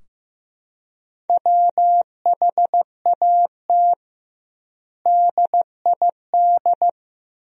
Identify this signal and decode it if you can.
Morse code, heard as a keyed tone, about 16 wpm, 700 Hz. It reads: WHAT DID
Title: 2x Word-Spacing - Sentences from Top 200 Words - 15wpm